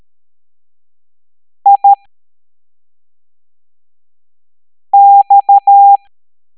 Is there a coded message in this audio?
Yes: Morse code from 1.7 to 6.0 s, heard as a keyed tone, about 13 wpm, spaced as for 3 wpm, 800 Hz. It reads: IX